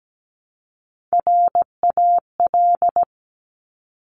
Morse code, heard as a keyed tone, about 17 wpm, 700 Hz. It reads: RAL
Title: Top Three Letter Combinations in Words 17wpm